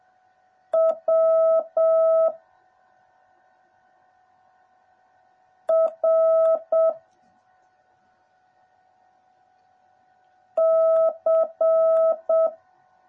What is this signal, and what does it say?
Morse code, heard as a keyed tone, about 7 wpm, 650 Hz. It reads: W R C
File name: jh8cbh / 2021cw / cw09a